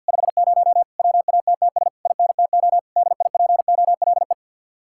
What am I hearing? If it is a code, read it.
Morse code, heard as a keyed tone, about 64 wpm, 700 Hz. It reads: 50 WATTS INTO DIPOLE